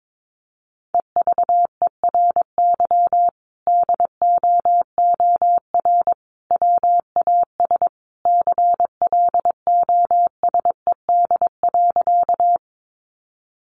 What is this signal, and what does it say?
Morse code, heard as a keyed tone, about 22 wpm, 700 Hz. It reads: EVERY DOOR WAS CLOSED.
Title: Sentences from Top 1400 Words 22wpm